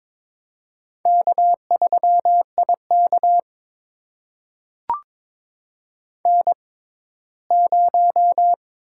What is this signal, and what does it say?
Morse code, heard as a keyed tone, about 22 wpm, spaced as for 15 wpm, 700 Hz. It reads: K3IK N0
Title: USA Call Signs - 4 Characters - 6x Character Spacing - 22wpm